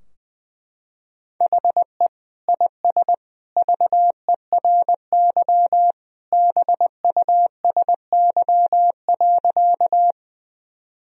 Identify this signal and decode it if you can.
Morse code, heard as a keyed tone, about 21 wpm, 700 Hz. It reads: HE IS VERY BUSY.
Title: Sentences from Top 1000 Words - Review Part 3 - 20wpm